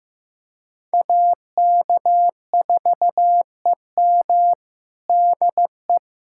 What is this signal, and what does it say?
Morse code, heard as a keyed tone, about 15 wpm, 700 Hz. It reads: AK4EM DE
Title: USA Callsign DE Callsign - 5 Characters - 15wpm